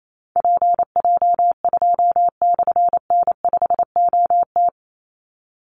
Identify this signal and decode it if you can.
Morse code, heard as a keyed tone, about 28 wpm, 700 Hz. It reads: PJ2/N5OT
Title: Call Signs - 28wpm